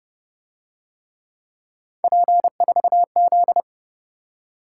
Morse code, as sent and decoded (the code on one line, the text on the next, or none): .--. ....- --..
P4Z